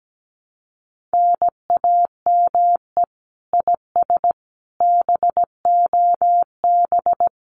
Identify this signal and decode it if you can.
Morse code, heard as a keyed tone, about 17 wpm, 700 Hz. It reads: NAME IS BOB
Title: Common QSO Phrases 17wpm